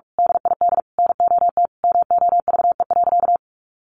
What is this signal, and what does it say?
Morse code, heard as a keyed tone, about 46 wpm, 700 Hz. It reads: DID NOT MOVE.
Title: A Sound Of Thunder 45wpm